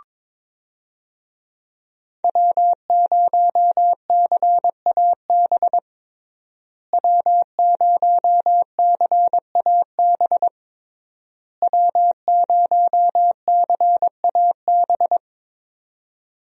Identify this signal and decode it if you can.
Morse code, heard as a keyed tone, about 22 wpm, 700 Hz. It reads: W0CAB W0CAB W0CAB
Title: USA Call Signs - 5 Characters - Repeated 3x - 22wpm